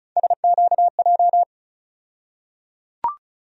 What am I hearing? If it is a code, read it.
Morse code, heard as a keyed tone, about 36 wpm, 700 Hz. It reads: SQJ
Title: USA Call Sign Postfixes - 3 Letters 35wpm